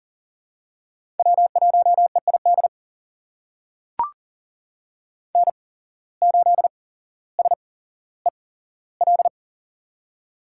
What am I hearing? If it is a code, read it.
Morse code, heard as a keyed tone, about 39 wpm, spaced as for 17 wpm, 700 Hz. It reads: W1EID N8SEL
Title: USA Call Signs - 8x Character Spacing - 40wpm